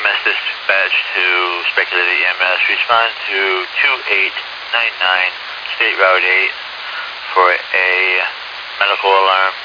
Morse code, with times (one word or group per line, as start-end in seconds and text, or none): none